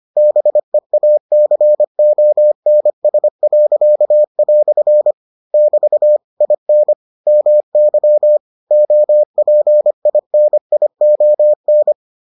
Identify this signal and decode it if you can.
Morse code, heard as a keyed tone, about 25 wpm, 600 Hz. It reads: BEACONS." <BT> IN MY OPINION